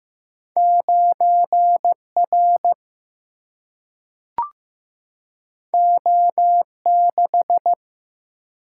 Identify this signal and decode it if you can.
Morse code, heard as a keyed tone, about 15 wpm, 700 Hz. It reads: 9R O6
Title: Single-Letter Single-Number 15wpm